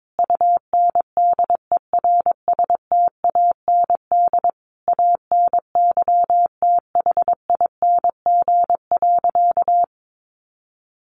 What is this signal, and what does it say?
Morse code, heard as a keyed tone, about 22 wpm, 700 Hz. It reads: UNDERSTAND ANYTHING.
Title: Sentences from Top 1000 Words 22wpm